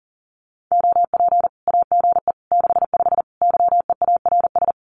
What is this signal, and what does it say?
Morse code, heard as a keyed tone, about 40 wpm, 700 Hz. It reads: OP AGE 65 YEARS
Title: Common QSO Phrases 40wpm